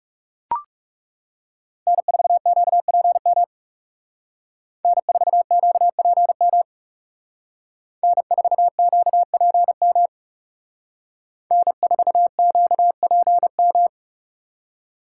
Morse code, heard as a keyed tone, about 35 wpm, 700 Hz. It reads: N4QPM N4QPM N4QPM N4QPM
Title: Speed Racing - USA 1x3 Call Signs 30wpm